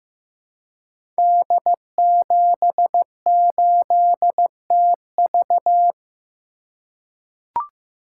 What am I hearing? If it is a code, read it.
Morse code, heard as a keyed tone, about 15 wpm, 700 Hz. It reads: D78TV
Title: Code Groups - 5 Letter:Numbers - 15wpm